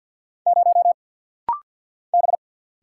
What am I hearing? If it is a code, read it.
Morse code, heard as a keyed tone, about 50 wpm, 700 Hz. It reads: 0 B
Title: Single Letter-Number Alt - Rapid-Fire 50wpm